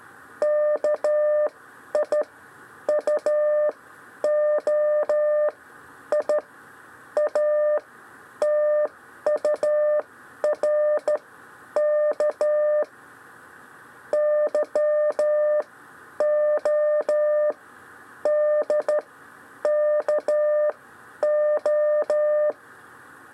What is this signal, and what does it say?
Morse code, heard as a keyed tone, about 13 wpm, spaced as for 9 wpm, 600 Hz. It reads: KIUOI ATURK YODKO
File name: 6 lezione cw 1